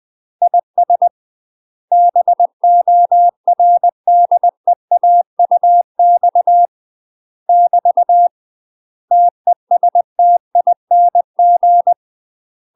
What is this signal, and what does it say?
Morse code, heard as a keyed tone, about 20 wpm, 700 Hz. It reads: IS BORDEAUX <BT> TESTING
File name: UK MASTER QSOs TRAINING / 20WPM / UK MASTER QSOs 0067